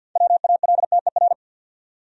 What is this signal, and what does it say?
Morse code, heard as a keyed tone, about 47 wpm, 700 Hz. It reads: WALTER
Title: USA Common Names - 50wpm